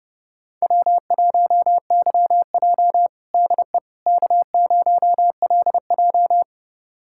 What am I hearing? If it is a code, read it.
Morse code, heard as a keyed tone, about 30 wpm, 700 Hz. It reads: W1YJ DE K0LJ
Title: USA Callsign DE Callsign - 4 Characters - 30wpm